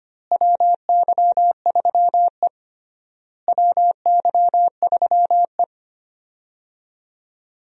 Morse code, heard as a keyed tone, about 25 wpm, 700 Hz. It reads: WY3E WY3E
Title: USA Call Signs - 4 Characters - Repeated 3x - 25wpm